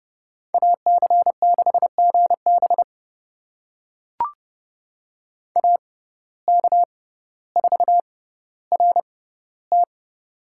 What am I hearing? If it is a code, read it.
Morse code, heard as a keyed tone, about 30 wpm, spaced as for 15 wpm, 700 Hz. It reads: AC6GB AK4RT
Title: USA Call Signs - 5 Characters - 6x Character Spacing - 30wpm